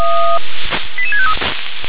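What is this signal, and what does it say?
Five-tone selective call (ZVEI), heard as quick runs of five tones, tones of about 70 ms. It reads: 88653
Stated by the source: ZVEI five-tone call received on SDR.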